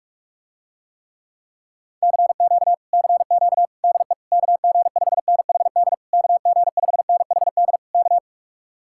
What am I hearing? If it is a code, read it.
Morse code, heard as a keyed tone, about 51 wpm, 700 Hz. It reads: CQ CQ DE KG5NHD KG5NHD K